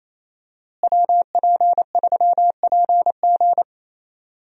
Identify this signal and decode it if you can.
Morse code, heard as a keyed tone, about 27 wpm, 700 Hz. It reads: WP3PG
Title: USA Call Signs - 5 Characters - Repeated 3x - 28wpm